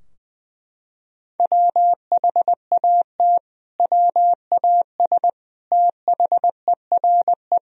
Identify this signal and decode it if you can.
Morse code, heard as a keyed tone, about 21 wpm, 700 Hz. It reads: WHAT WAS THERE
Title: Sentences from Top 1000 Words - Review Part 3 - 20wpm